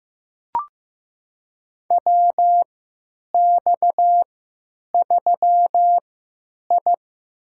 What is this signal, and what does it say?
Morse code, heard as a keyed tone, about 15 wpm, spaced as for 8 wpm, 700 Hz. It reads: WX3I
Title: USA Call Signs - 3x Character Spacing - 15wpm